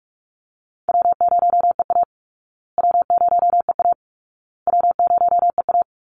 Morse code, heard as a keyed tone, about 44 wpm, 700 Hz. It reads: W0EA W0EA W0EA
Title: Call Signs Repeated 2x - 45wpm